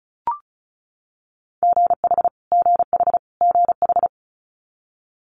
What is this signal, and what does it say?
Morse code, heard as a keyed tone, about 36 wpm, 700 Hz. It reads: GH GH GH
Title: Multi-letter Phonemes Repeated 3x 35wpm